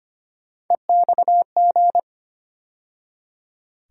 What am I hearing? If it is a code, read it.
Morse code, heard as a keyed tone, about 25 wpm, 700 Hz. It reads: EXG
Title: USA Call Sign Postfixes - 3 Letters 25wpm